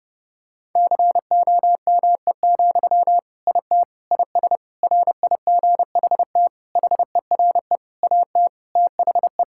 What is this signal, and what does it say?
Morse code, heard as a keyed tone, about 30 wpm, 700 Hz. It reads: COME, IT IS RIGHT HERE AT THE
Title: Sentences from Top 1600 Words 30wpm